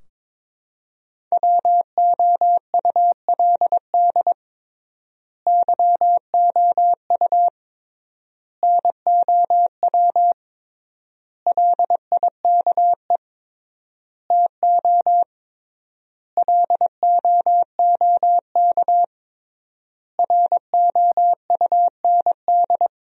Morse code, heard as a keyed tone, about 22 wpm, 700 Hz. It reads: WOULD YOU NOW LIKE TO LOOK ROUND
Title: 3x Word-Spacing - Sentences from Top 200 Words - 22wpm